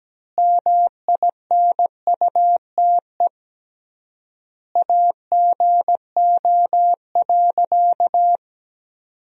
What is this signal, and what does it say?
Morse code, heard as a keyed tone, about 17 wpm, 700 Hz. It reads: MINUTE AGO.
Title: 3x Word-Spacing - Sentences from Top 500 Words - 17wpm